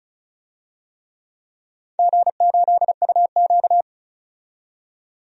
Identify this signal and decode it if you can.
Morse code, heard as a keyed tone, about 34 wpm, 700 Hz. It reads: G8UQ
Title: DXCC-100 Call Signs - 4 Characters - 35wpm